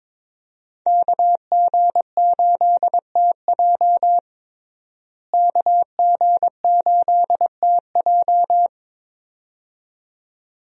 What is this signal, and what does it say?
Morse code, heard as a keyed tone, about 21 wpm, 700 Hz. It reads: KG8TJ KG8TJ
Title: USA Call Signs - 5 Characters - Repeated 2x - 22wpm